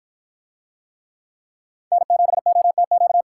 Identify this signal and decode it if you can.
Morse code, heard as a keyed tone, about 53 wpm, 700 Hz. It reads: N7OTQ